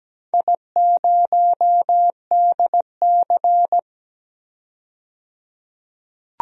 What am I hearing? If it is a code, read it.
Morse code, heard as a keyed tone, about 17 wpm, 700 Hz. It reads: I0DC
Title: DXCC-100 Call Signs - 4 Characters - 17wpm